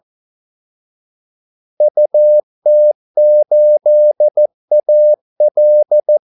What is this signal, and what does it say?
Morse code, heard as a keyed tone, about 14 wpm, 600 Hz. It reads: UT8AL
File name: Callsigns-14wpm